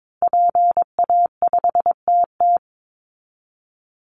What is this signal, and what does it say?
Morse code, heard as a keyed tone, about 22 wpm, 700 Hz. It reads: PA5TT
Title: Call Signs Repeated 2x - 22wpm